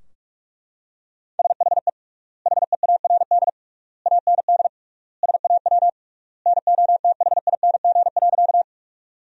Morse code, heard as a keyed tone, about 42 wpm, 700 Hz. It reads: SHE HEARD AND SAW NOTHING.
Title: 3x Word-Spacing - Sentences from Top 500 Words - 45wpm